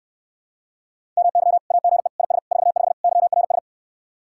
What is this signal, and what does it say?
Morse code, heard as a keyed tone, about 69 wpm, 700 Hz. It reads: MY AGE IS 45 YRS